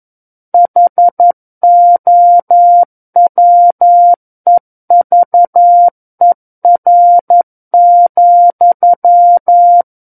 Morse code, heard as a keyed tone, about 11 wpm, 700 Hz. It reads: HOWEVER,
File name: Book One / 02-Chapter 2 - 11 WPM